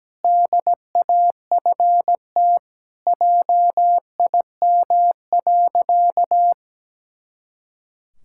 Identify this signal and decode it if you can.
Morse code, heard as a keyed tone, about 17 wpm, 700 Hz. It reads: DAFT JIM.
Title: Pangram Challenge v2 - 17wpm